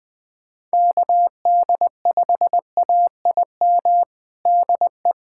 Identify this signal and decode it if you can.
Morse code, heard as a keyed tone, about 20 wpm, 700 Hz. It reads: KD5AIM DE
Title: USA Callsign DE Callsign 20wpm